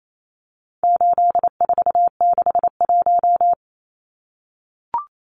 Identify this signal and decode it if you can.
Morse code, heard as a keyed tone, about 28 wpm, 700 Hz. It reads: 8461
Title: Numbers - 4-Digit 28wpm